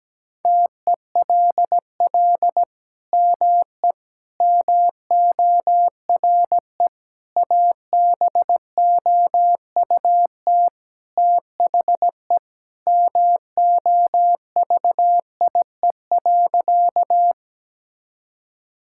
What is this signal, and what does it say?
Morse code, heard as a keyed tone, about 17 wpm, 700 Hz. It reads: TELL ME MORE ABOUT THE MOVIE.
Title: Sentences from Top 1000 Words - Review Part 5 - 17wpm